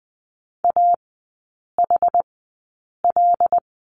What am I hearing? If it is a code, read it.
Morse code, heard as a keyed tone, about 20 wpm, spaced as for 8 wpm, 700 Hz. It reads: AHL
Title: 074 Lesson 13e - Letter L - Three Characters 20wpm